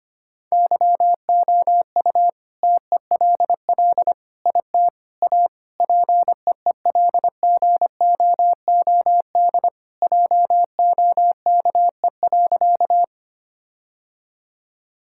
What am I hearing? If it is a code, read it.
Morse code, heard as a keyed tone, about 25 wpm, 700 Hz. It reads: YOU TELL IT A PEELGOOD JOKE.
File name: practice / 25wpm / All-Age Appropriate Jokes - 25wpm